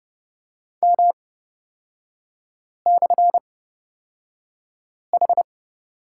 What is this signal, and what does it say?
Morse code, heard as a keyed tone, about 29 wpm, spaced as for 5 wpm, 690 Hz. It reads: M/H
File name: Lesson 37e - Letter x - Three Characters 30wpm